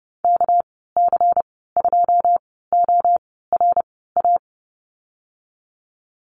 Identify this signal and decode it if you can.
Morse code, heard as a keyed tone, about 30 wpm, spaced as for 17 wpm, 700 Hz. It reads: KC2ORA
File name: Lesson 37g - Letter x - Call Signs 30wpm